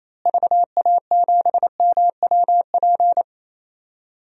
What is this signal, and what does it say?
Morse code, heard as a keyed tone, about 28 wpm, 700 Hz. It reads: VA7MWP
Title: DXCC-100 Call Signs - 6 Characters - Repeated 2x - 28wpm